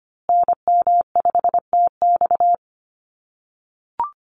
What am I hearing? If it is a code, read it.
Morse code, heard as a keyed tone, about 25 wpm, 700 Hz. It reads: NM5TX
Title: USA Call Signs Repeated 2x 25wpm